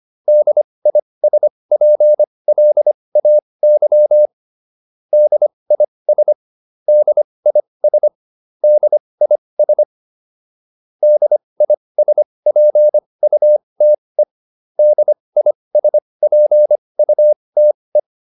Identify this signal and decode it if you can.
Morse code, heard as a keyed tone, about 25 wpm, spaced as for 20 wpm, 600 Hz. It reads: DISPLAY DIS DIS DIS DISPUTE DISPUTE
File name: DIS7 20